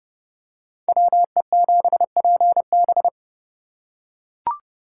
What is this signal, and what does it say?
Morse code, heard as a keyed tone, about 30 wpm, 700 Hz. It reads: WE7PB